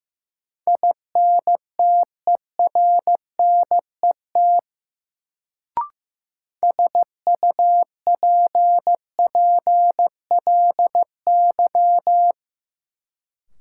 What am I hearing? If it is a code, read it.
Morse code, heard as a keyed tone, about 15 wpm, 700 Hz. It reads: INTERNET SUPPLY
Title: Sentences from Top 1000 Words - Review Part 4 - New Word Drill - 15wpm